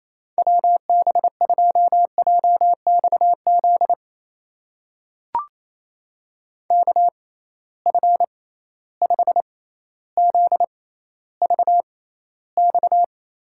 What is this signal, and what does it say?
Morse code, heard as a keyed tone, about 28 wpm, spaced as for 14 wpm, 700 Hz. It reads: WB2JXZ KF5ZVX
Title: USA Call Signs - 6x Character Spacing - 28wpm